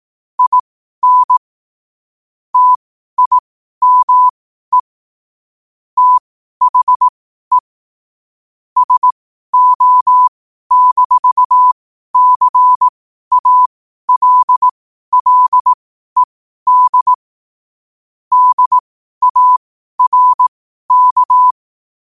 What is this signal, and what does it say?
Morse code, heard as a keyed tone, about 18 wpm, spaced as for 12 wpm, 1000 Hz. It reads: IN TIME THE SO-CALLED DARK
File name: qotdfiles / 12wpm